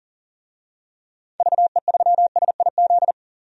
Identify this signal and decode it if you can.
Morse code, heard as a keyed tone, about 40 wpm, 700 Hz. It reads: VE3SIZ
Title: DXCC-100 Call Signs - 6 Characters - 40wpm